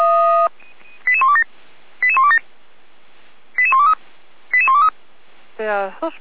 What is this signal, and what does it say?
Five-tone selective call (ZVEI), heard as quick runs of five tones, tones of about 70 ms, with speech. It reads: 88127 88127 88123 88123